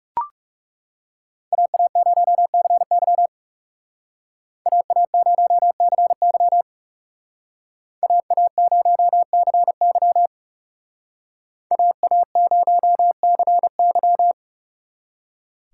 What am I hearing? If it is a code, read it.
Morse code, heard as a keyed tone, about 39 wpm, 700 Hz. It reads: AA0CY AA0CY AA0CY AA0CY